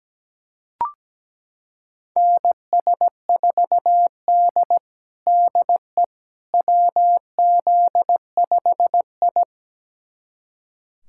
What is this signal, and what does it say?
Morse code, heard as a keyed tone, about 17 wpm, 700 Hz. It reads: NS4D DE WZ5I
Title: USA Callsign DE Callsign - 4 Characters - 17wpm